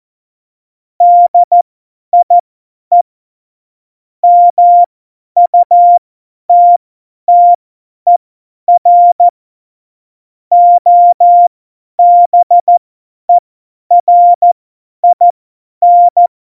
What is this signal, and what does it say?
Morse code, heard as a keyed tone, about 14 wpm, spaced as for 10 wpm, 700 Hz. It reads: DIE MUTTER OBERIN
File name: Morsekurs / Witze / Witz 12. Januar 2021, 50 F70